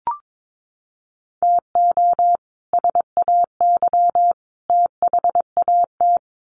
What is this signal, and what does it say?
Morse code, heard as a keyed tone, about 22 wpm, 700 Hz. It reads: TO SAY THAT
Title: Sets of 3 Words 22wpm